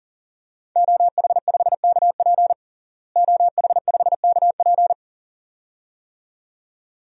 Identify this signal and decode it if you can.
Morse code, heard as a keyed tone, about 40 wpm, 700 Hz. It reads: OH5KP OH5KP
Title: DXCC-100 Call Signs - 5 Characters - Repeated 2x - 40wpm